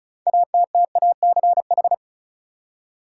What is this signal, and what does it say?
Morse code, heard as a keyed tone, about 36 wpm, 700 Hz. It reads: ATTACH
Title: Six Letter Words - 35wpm